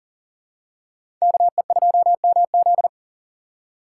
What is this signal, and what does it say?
Morse code, heard as a keyed tone, about 40 wpm, 700 Hz. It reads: KE2MZ
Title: USA Call Signs - 3x Character Spacing - 40wpm